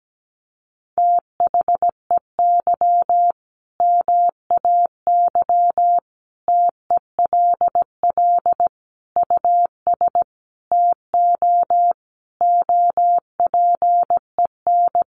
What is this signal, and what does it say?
Morse code, heard as a keyed tone, about 17 wpm, 700 Hz. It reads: THEY MAY TELL US TO OPEN
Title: Sentences from Top 300 Words 17wpm